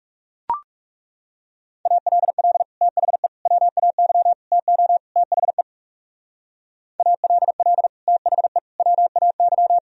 Morse code, heard as a keyed tone, about 36 wpm, 700 Hz. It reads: ALL THE WAY TO THE ALL THE WAY
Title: Speed-Racing - Sets of 5 Words 30wpm